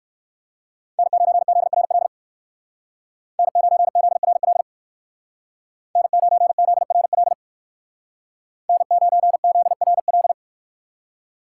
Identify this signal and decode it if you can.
Morse code, heard as a keyed tone, about 56 wpm, 700 Hz. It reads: N9ZRL N9ZRL N9ZRL N9ZRL
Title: Speed Racing - USA 1x3 Call Signs 45wpm